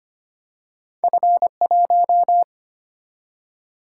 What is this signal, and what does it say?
Morse code, heard as a keyed tone, about 25 wpm, 700 Hz. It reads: F 1
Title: Single-Letter Single-Number 25wpm